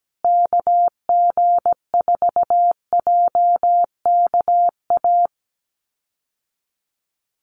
Morse code, heard as a keyed tone, about 17 wpm, 700 Hz. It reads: KG4JKA
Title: USA Call Signs Repeated 2x 17wpm